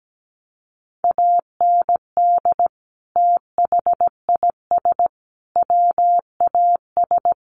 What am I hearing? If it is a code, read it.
Morse code, heard as a keyed tone, about 17 wpm, 700 Hz. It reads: AND THIS WAS
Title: Sentences from Top 1100 Words 17wpm